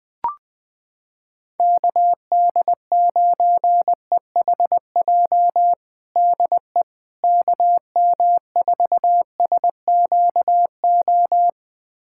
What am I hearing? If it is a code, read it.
Morse code, heard as a keyed tone, about 20 wpm, 700 Hz. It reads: KD9EHJ DE KM4SQO
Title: USA Callsign DE Callsign 20wpm